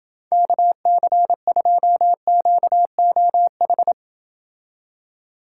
Morse code, heard as a keyed tone, about 28 wpm, 700 Hz. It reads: KC2QOH